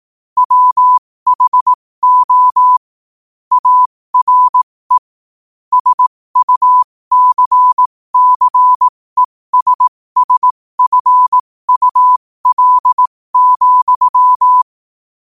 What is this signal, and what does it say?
Morse code, heard as a keyed tone, about 18 wpm, spaced as for 15 wpm, 1000 Hz. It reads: WHO ARE SUCCESSFUL,
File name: qotdfiles / 15wpm